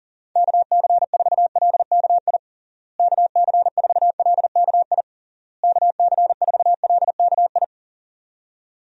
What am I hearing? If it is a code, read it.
Morse code, heard as a keyed tone, about 40 wpm, 700 Hz. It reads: KC4LKI KC4LKI KC4LKI
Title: USA Call Signs - 6 Characters - Repeated 3x - 40wpm